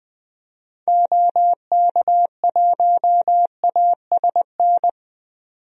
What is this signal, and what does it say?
Morse code, heard as a keyed tone, about 20 wpm, 700 Hz. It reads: OK1ASN
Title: DXCC-100 Call Signs - 6 Characters - Repeated 3x - 20wpm